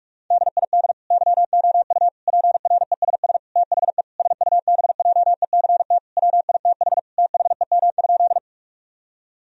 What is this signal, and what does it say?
Morse code, heard as a keyed tone, about 46 wpm, 700 Hz. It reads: DID YOU PRESS THE SUBJECT WITH THEM?
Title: Sentences from Top 1000 Words - Review Part 1 - 45wpm